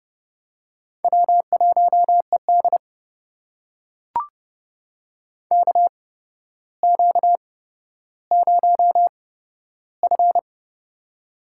Morse code, heard as a keyed tone, about 30 wpm, spaced as for 13 wpm, 700 Hz. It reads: W1ED KQ0F
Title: USA Call Signs - 4 Characters - 8x Character Spacing - 30wpm